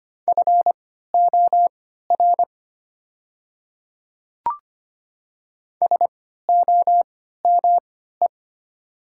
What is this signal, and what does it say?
Morse code, heard as a keyed tone, about 25 wpm, spaced as for 14 wpm, 700 Hz. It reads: FOR SOME
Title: 3x Character Spacing - Top 100 Words - 25wpm